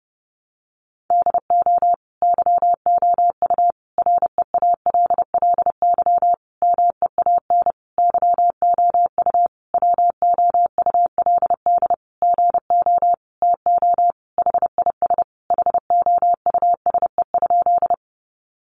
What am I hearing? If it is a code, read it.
Morse code, heard as a keyed tone, about 30 wpm, 700 Hz. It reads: DO YOU REALLY MEAN YOU WOULD GO TO HIS HOUSE?
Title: Sentences from Top 600 Words 30wpm